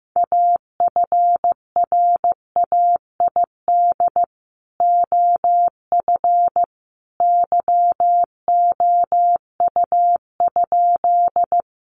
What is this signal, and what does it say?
Morse code, heard as a keyed tone, about 15 wpm, 700 Hz. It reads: AFRAID OF YOU?
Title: Sentences from Top 1500 Words 15wpm